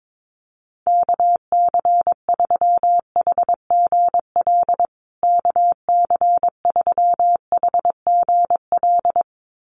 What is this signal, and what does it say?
Morse code, heard as a keyed tone, about 22 wpm, 700 Hz. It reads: KC3HGL KC3HGL
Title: USA Call Signs Repeated 2x 22wpm